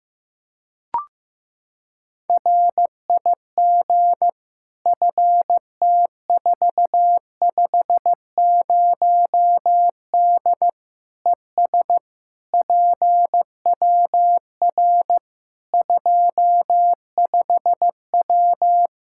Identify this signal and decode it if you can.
Morse code, heard as a keyed tone, about 15 wpm, 700 Hz. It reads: RIG FT450D ES PWR 25W
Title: Practice QSO - Exchange 2 - Info - 15wpm